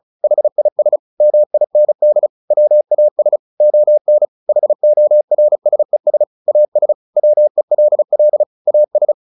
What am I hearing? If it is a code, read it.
Morse code, heard as a keyed tone, about 34 wpm, 600 Hz. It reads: HIS MIND WAS ON HORSES AS WELL AS
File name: the old man and the sea / the old man and the sea 1 35wpm